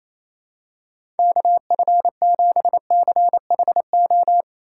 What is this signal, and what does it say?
Morse code, heard as a keyed tone, about 28 wpm, 700 Hz. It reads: KF7CHO
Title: USA Call Signs - 4x Character Spacing - 28wpm